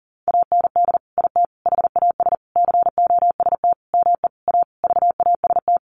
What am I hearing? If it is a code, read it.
Morse code, heard as a keyed tone, about 37 wpm, 700 Hz. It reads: AND IT HAS COST ME A VAST
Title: Sentences from Top 1300 Words 40wpm